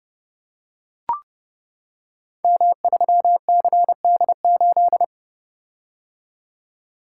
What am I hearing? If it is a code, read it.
Morse code, heard as a keyed tone, about 30 wpm, 700 Hz. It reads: M3CD8
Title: Code Groups - 5 Letter:Numbers - 30wpm